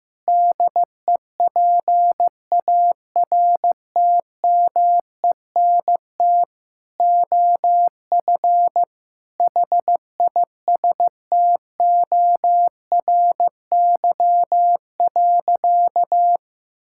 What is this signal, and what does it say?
Morse code, heard as a keyed tone, about 16 wpm, 700 Hz. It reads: DEPARTMENT OF HISTORY.
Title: Sentences from Top 1000 Words - Review Part 5 - 15wpm